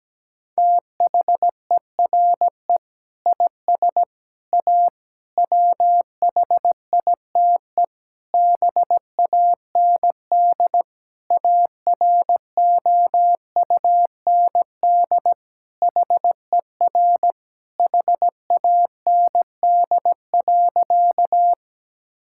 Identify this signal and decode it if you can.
Morse code, heard as a keyed tone, about 17 wpm, 700 Hz. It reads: THERE IS A WHITE BAND AROUND HER HAND.
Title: Sentences from Top 1600 Words 17wpm